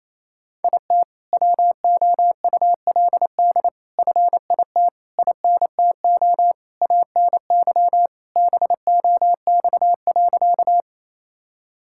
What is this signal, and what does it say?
Morse code, heard as a keyed tone, about 28 wpm, 700 Hz. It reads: IT WOULD FIT INTO ANY BOX.
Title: Sentences from Top 1000 Words - Review Part 5 - 28wpm